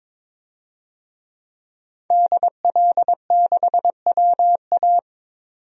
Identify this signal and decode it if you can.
Morse code, heard as a keyed tone, about 22 wpm, 700 Hz. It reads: DL6WA
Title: DXCC-100 Call Signs - 5 Characters - 22wpm